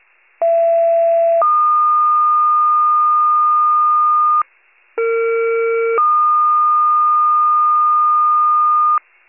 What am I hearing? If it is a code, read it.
Two-tone page, heard as a one-second tone, then a three-second tone, about 670 then 1220 Hz, about 460 then 1220 Hz. It is heard from 0.4 to 9.0 s.
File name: pager / 39 RESCUE / wcalert